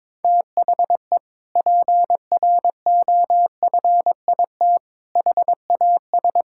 Morse code, heard as a keyed tone, about 22 wpm, 700 Hz. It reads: THE PROFIT HAS
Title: Sentences from Top 1000 Words - Review Part 5 - 22wpm